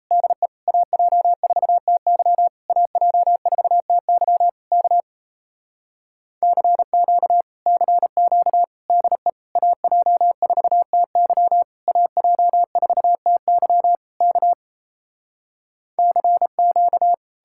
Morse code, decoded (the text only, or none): DE AJ4TY AJ4TY K CQ CQ DE AJ4TY AJ4TY K CQ